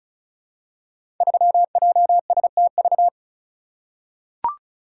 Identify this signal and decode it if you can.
Morse code, heard as a keyed tone, about 36 wpm, 700 Hz. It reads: 3JSTV
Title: Code Groups - 5 Letter:Numbers - 35wpm